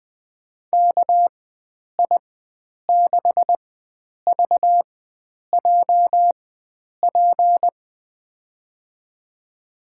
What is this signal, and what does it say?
Morse code, heard as a keyed tone, about 20 wpm, spaced as for 9 wpm, 700 Hz. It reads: KI6VJP
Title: USA Call Signs - 4x Character Spacing - 20wpm